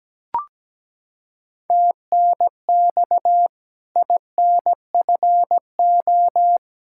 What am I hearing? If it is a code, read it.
Morse code, heard as a keyed tone, about 17 wpm, 700 Hz. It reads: TNX INFO
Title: Practice QSO - Exchange 4 - Ending - 17wpm